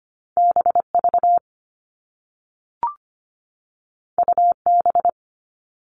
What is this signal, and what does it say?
Morse code, heard as a keyed tone, about 25 wpm, 700 Hz. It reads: BV UB
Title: All Cut Numbers - 2-Digit 25wpm